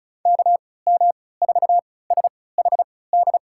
Morse code, heard as a keyed tone, about 36 wpm, spaced as for 20 wpm, 700 Hz. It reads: KM4SHD
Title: USA Call Signs - 6 Characters - 3x Character Spacing - 35wpm